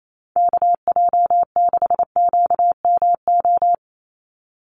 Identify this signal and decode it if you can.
Morse code, heard as a keyed tone, about 28 wpm, 700 Hz. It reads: KJ6QMO
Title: USA Call Signs 28wpm